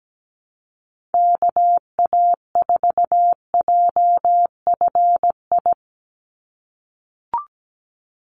Morse code, heard as a keyed tone, about 17 wpm, 700 Hz. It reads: KA4JFI
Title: USA Call Signs 17wpm